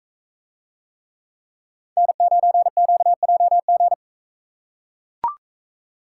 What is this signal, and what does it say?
Morse code, heard as a keyed tone, about 44 wpm, 700 Hz. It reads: N9QJG